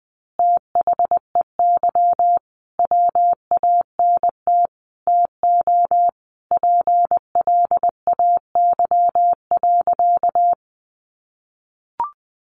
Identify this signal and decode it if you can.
Morse code, heard as a keyed tone, about 20 wpm, 700 Hz. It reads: THEY WANT TO PLAY.
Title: Easy Phrases 20wpm